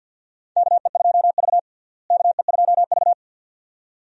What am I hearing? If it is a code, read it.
Morse code, heard as a keyed tone, about 52 wpm, 700 Hz. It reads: KE2V KE2V
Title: USA Call Signs - 4 Characters - Repeated 2x - 50wpm